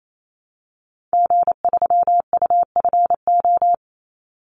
Morse code, heard as a keyed tone, about 28 wpm, 700 Hz. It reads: G3UFO
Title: Call Signs Repeated 3x - 28wpm